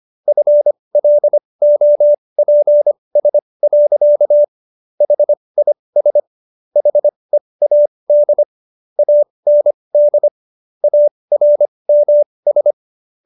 Morse code, heard as a keyed tone, about 25 wpm, spaced as for 20 wpm, 590 Hz. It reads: FLOPS. HIS HEAD AND ARMS